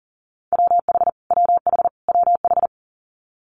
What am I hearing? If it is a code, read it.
Morse code, heard as a keyed tone, about 39 wpm, 700 Hz. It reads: WH WH WH
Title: Multi-letter Phonemes Repeated 3x 40wpm